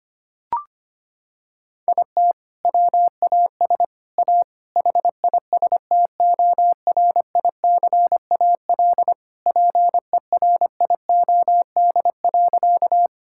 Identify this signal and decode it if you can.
Morse code, heard as a keyed tone, about 25 wpm, 700 Hz. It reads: IT WAS A HISTORICAL PERIOD.
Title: Sentences from Top 1000 Words - Review Part 5 - 25wpm